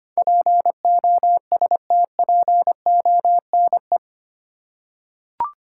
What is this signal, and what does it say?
Morse code, heard as a keyed tone, about 25 wpm, 700 Hz. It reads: POSTPONE
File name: Eight Letter Words - 25wpm